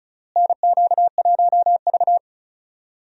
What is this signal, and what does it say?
Morse code, heard as a keyed tone, about 36 wpm, 700 Hz. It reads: NQ1V
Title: USA Call Signs - 4 Characters - Repeated 3x - 35wpm